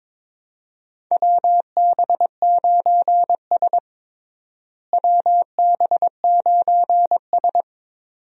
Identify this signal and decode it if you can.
Morse code, heard as a keyed tone, about 22 wpm, 700 Hz. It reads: WB9S WB9S
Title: USA Call Signs - 4 Characters - Repeated 2x - 22wpm